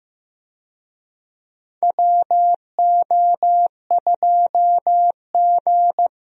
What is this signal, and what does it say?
Morse code, heard as a keyed tone, about 15 wpm, 700 Hz. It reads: WO2G
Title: USA Call Signs - 4 Characters - Repeated 2x - 15wpm